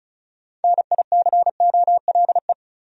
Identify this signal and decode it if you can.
Morse code, heard as a keyed tone, about 36 wpm, 700 Hz. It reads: NICOLE
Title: USA Common Names - 35wpm